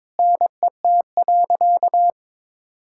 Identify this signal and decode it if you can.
Morse code, heard as a keyed tone, about 22 wpm, 700 Hz. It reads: NET.
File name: practice / 22wpm / Sentences with 8-Word Phrases - 22wpm